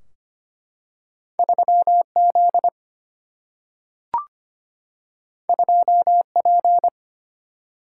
Morse code, heard as a keyed tone, about 25 wpm, 700 Hz. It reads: 3Z 2P
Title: Single-Letter Single-Number 25wpm